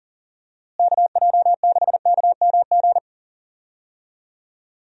Morse code, heard as a keyed tone, about 40 wpm, 700 Hz. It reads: KJ6KMG
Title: USA Call Signs - 6 Characters - Repeated 2x - 40wpm